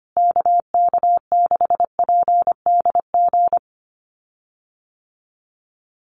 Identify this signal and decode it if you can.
Morse code, heard as a keyed tone, about 25 wpm, 700 Hz. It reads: KK6PDG
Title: USA Call Signs Repeated 3x 25wpm